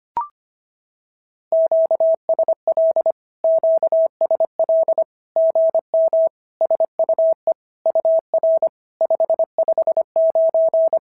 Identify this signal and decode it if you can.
Morse code, heard as a keyed tone, about 25 wpm, 650 Hz. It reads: QSL QSL GM SUE UR 559
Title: SOTA QSO Practice 25wpm